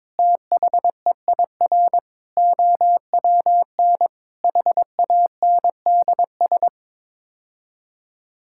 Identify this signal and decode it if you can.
Morse code, heard as a keyed tone, about 21 wpm, 700 Hz. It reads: THEIR OWN HANDS
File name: practice / 22wpm / Sets of 8 Words - 22wpm